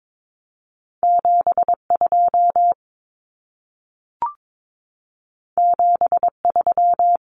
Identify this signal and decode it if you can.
Morse code, heard as a keyed tone, about 22 wpm, 700 Hz. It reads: 72 73
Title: Numbers - 2-Digit 22wpm